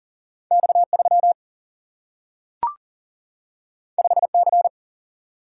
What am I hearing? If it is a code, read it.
Morse code, heard as a keyed tone, about 40 wpm, 700 Hz. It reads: X 3 5 C
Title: Single-Letter Single-Number 40wpm